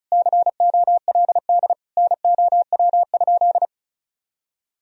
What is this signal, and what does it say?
Morse code, heard as a keyed tone, about 34 wpm, 700 Hz. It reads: COLD NOW?
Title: Sentences from Top 1000 Words - Review Part 2 - 35wpm